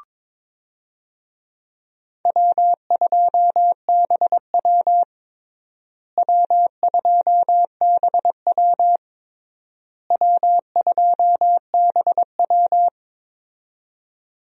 Morse code, heard as a keyed tone, about 22 wpm, 700 Hz. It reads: W2BW W2BW W2BW
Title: USA Call Signs - 4 Characters - Repeated 3x - 22wpm